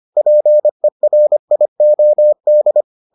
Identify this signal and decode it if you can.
Morse code, heard as a keyed tone, about 25 wpm, 600 Hz. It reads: PERIOD